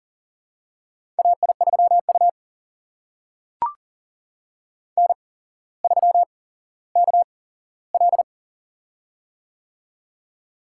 Morse code, heard as a keyed tone, about 40 wpm, spaced as for 17 wpm, 700 Hz. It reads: AI3U N3KL
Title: USA Call Signs - 8x Character Spacing - 40wpm